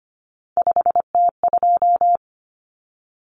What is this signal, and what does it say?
Morse code, heard as a keyed tone, about 25 wpm, 700 Hz. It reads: 5T2
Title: Common Cut Numbers - 3-Digit 25wpm